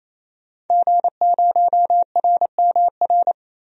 Morse code, heard as a keyed tone, about 28 wpm, 700 Hz. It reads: G0RMR
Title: DXCC-100 Call Signs - 5 Characters - Repeated 3x - 28wpm